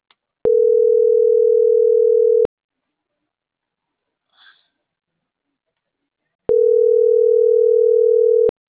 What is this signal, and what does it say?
Ringback tone (North America), 440 + 480 Hz.